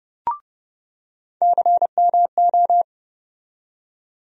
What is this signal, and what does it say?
Morse code, heard as a keyed tone, about 29 wpm, 700 Hz. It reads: CMO
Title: USA Call Sign Postfixes - 3 Letters 30wpm